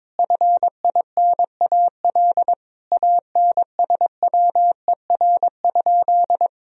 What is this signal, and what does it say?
Morse code, heard as a keyed tone, about 22 wpm, 700 Hz. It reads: FINAL ANSWER?
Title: Sentences from Top 1000 Words - Review Part 2 - 22wpm